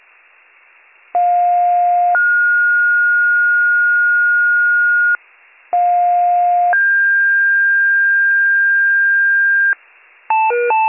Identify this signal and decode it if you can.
Two-tone page, heard as a one-second tone, then a three-second tone, about 710 then 1430 Hz, about 710 then 1600 Hz, with noise. It is heard from 1.1 to 9.7 s.